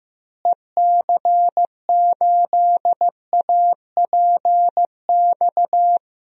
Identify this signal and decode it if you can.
Morse code, heard as a keyed tone, about 15 wpm, 700 Hz. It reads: EC8APX
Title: DXCC-100 Call Signs - 6 Characters - Repeated 3x - 15wpm